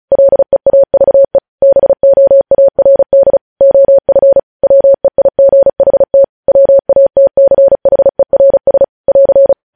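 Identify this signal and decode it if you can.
Morse code, heard as a keyed tone, about 36 wpm, 550 Hz. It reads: LEAVE BOARD OF WEIGHT WATCHERS <AR>
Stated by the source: Morse code news headlines sent at 35 wpm.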